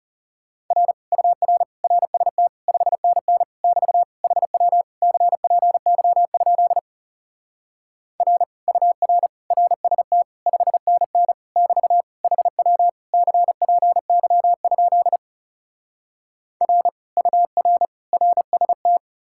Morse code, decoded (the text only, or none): R UR RST 5NN <BT> HW CPY? R UR RST 5NN <BT> HW CPY? R UR RST